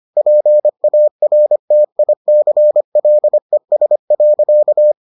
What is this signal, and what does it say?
Morse code, heard as a keyed tone, about 25 wpm, 600 Hz. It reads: PARTICLES.